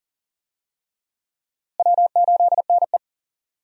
Morse code, heard as a keyed tone, about 40 wpm, 700 Hz. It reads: W8NE
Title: USA Call Signs - 4 Characters - Repeated 3x - 40wpm